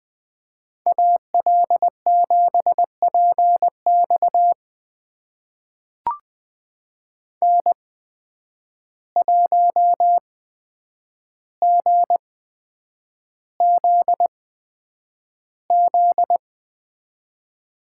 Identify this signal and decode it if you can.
Morse code, heard as a keyed tone, about 20 wpm, spaced as for 9 wpm, 700 Hz. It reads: AL7PX N1GZZ
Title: USA Call Signs - 5 Characters - 8x Character Spacing - 20wpm